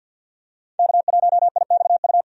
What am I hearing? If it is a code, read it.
Morse code, heard as a keyed tone, about 52 wpm, 700 Hz. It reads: K1IXU
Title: USA Call Signs - 5 Characters - 50wpm